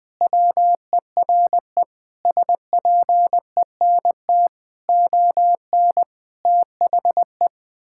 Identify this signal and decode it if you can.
Morse code, heard as a keyed tone, about 20 wpm, 700 Hz. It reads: WERE SPENT ON THE
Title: Sentences with 5-Word Phrases - 20wpm